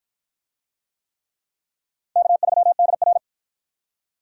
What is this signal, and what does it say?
Morse code, heard as a keyed tone, about 53 wpm, 700 Hz. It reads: K3DR